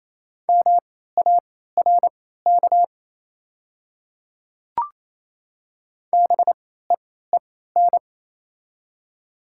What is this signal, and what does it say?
Morse code, heard as a keyed tone, about 28 wpm, spaced as for 16 wpm, 700 Hz. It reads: MARK BEEN
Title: 3x Character Spacing - Top 300 Words - 28wpm